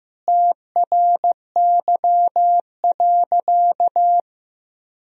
Morse code, heard as a keyed tone, about 15 wpm, 700 Hz. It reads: TRY.
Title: Sentences from Top 1000 Words - Review Part 4 - 15wpm